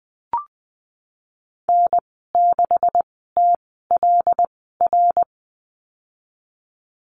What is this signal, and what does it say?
Morse code, heard as a keyed tone, about 20 wpm, spaced as for 15 wpm, 700 Hz. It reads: N6TLR
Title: 140 Lesson 23g - Letter f - Call Signs 20wpm